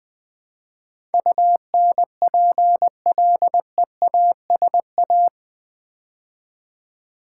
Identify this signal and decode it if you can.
Morse code, heard as a keyed tone, about 20 wpm, 700 Hz. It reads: UNPLEASA
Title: Build A Word - 4 Letters At A Time - 10-Letter Words - 20wpm